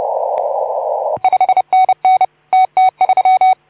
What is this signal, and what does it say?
Morse code, heard as a keyed tone, about 29 wpm, 2250 Hz, with noise. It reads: 5NN TT3
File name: Archiv / CUC181 / OK2ZC CUC 181